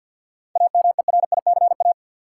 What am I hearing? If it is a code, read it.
Morse code, heard as a keyed tone, about 48 wpm, 700 Hz. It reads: AMERICA
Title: Sets of 5 Words - New Word Drill - 50wpm